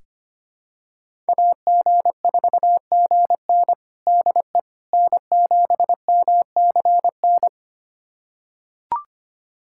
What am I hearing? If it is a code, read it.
Morse code, heard as a keyed tone, about 25 wpm, 700 Hz. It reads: AG4GN DE N7MCN
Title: USA Callsign DE Callsign - 5 Characters - 25wpm